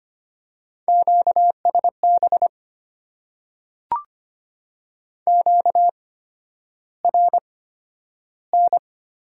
Morse code, heard as a keyed tone, about 25 wpm, spaced as for 11 wpm, 700 Hz. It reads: QSB QRN
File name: Top 90 QSO Elements - 8x Character Spacing - 25wpm